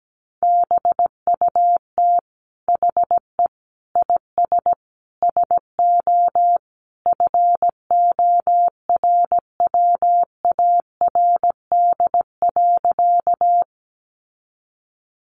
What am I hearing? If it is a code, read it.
Morse code, heard as a keyed tone, about 17 wpm, 700 Hz. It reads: BUT HE IS SO FORWARD.
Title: Sentences from Top 1300 Words 17wpm